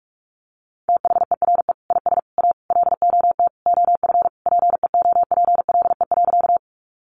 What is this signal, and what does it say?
Morse code, heard as a keyed tone, about 44 wpm, 700 Hz. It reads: THERE IS A LOT OF PEOPLE.
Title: Sentences from Top 500 Words 45wpm